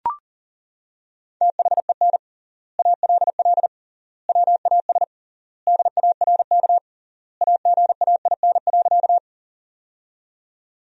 Morse code, heard as a keyed tone, about 40 wpm, 700 Hz. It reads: THEN ALL WAS DARK AGAIN.
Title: 3x Word-Spacing - Sentences from Top 500 Words - 40wpm